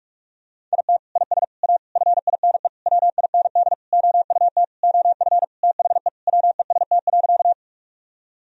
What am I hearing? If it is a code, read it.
Morse code, heard as a keyed tone, about 46 wpm, 700 Hz. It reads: IT IS A FINE WIND OUT OF THE WEST.
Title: Sentences from Top 1000 Words - Review Part 2 - 45wpm